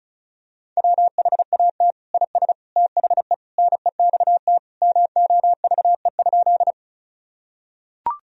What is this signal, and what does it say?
Morse code, heard as a keyed tone, about 34 wpm, 700 Hz. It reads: WHAT IS THE NEXT MOVE?
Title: Sentences from Top 1000 Words - Review Part 1 - 35wpm